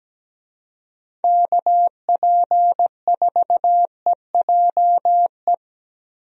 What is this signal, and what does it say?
Morse code, heard as a keyed tone, about 17 wpm, 700 Hz. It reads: KP4EJE
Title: DXCC-100 Call Signs - 6 Characters - Repeated 3x - 17wpm